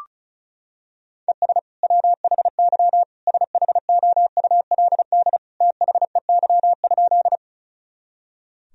Morse code, heard as a keyed tone, about 36 wpm, 700 Hz. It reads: ES WHY SHOULD THEY?
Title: Sentences from Top 300 Words and Abbreviations 35wpm